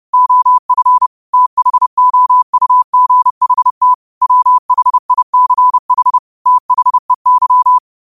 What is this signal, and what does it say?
Morse code, heard as a keyed tone, about 30 wpm, 1000 Hz. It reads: OF THOUGHT WHICH THEY
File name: qotdfiles / 30wpm